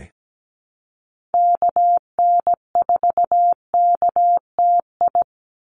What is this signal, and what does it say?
Morse code, heard as a keyed tone, about 17 wpm, 700 Hz. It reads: KN4KTI
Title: USA Call Signs 17wpm